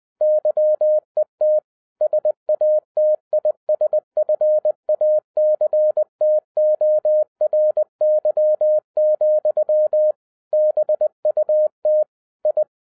Morse code, heard as a keyed tone, about 20 wpm, 610 Hz. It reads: YET SATISFACTORY, BUT I
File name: 2012-01-mp3 / n05-CAME BACK TO HAM RADIO AFTER 30 YEARS OF INTERVAL-20wpm